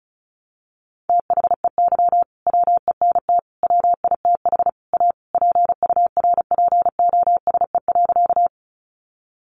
Morse code, heard as a keyed tone, about 34 wpm, 700 Hz. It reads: THEY WENT WITH A PURPOSE.
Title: Sentences from Top 800 Words 35wpm